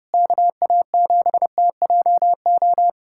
Morse code, heard as a keyed tone, about 30 wpm, 700 Hz. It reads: KA7TJO
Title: USA Callsign DE Callsign - 6 Characters - 30wpm